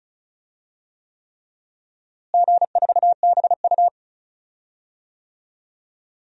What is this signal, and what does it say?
Morse code, heard as a keyed tone, about 34 wpm, 700 Hz. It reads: G4BU
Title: DXCC-100 Call Signs - 4 Characters - Repeated 3x - 35wpm